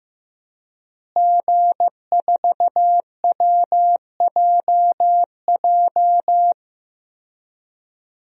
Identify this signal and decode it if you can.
Morse code, heard as a keyed tone, about 15 wpm, 700 Hz. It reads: G4WJJ
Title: DXCC-100 Call Signs - 5 Characters - Repeated 3x - 15wpm